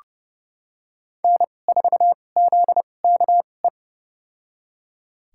Morse code, heard as a keyed tone, about 30 wpm, spaced as for 22 wpm, 700 Hz. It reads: N4ZKE
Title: USA Call Signs - 5 Characters - 2x Character Spacing - 30wpm